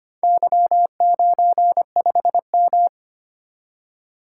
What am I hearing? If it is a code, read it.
Morse code, heard as a keyed tone, about 25 wpm, 700 Hz. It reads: Y95M
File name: practice / 25wpm / Four Letter-Number Combinations - 25wpm